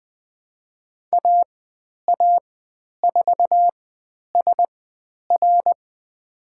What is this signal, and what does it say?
Morse code, heard as a keyed tone, about 20 wpm, spaced as for 10 wpm, 700 Hz. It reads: AA4SR
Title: 095 Lesson 16g - Letter c - Call Signs 20wpm